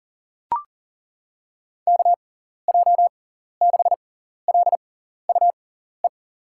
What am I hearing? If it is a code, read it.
Morse code, heard as a keyed tone, about 40 wpm, spaced as for 14 wpm, 700 Hz. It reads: KJ6LUE
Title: USA Call Signs - 6 Characters - 6x Character Spacing - 40wpm